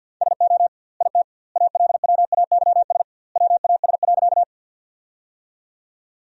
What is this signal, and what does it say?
Morse code, heard as a keyed tone, about 48 wpm, 700 Hz. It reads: SO IT ALWAYS WAS.
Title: 2x Word-Spacing - Sentences from Top 300 Words - 50wpm